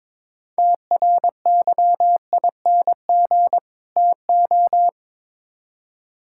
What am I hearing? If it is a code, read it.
Morse code, heard as a keyed tone, about 22 wpm, 700 Hz. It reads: TRYING TO
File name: Build A Phrase - 2-Word Phrases - 22wpm